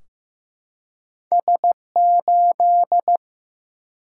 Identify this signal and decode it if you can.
Morse code, heard as a keyed tone, about 15 wpm, 700 Hz. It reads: S8
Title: Single-Letter Single-Number 15wpm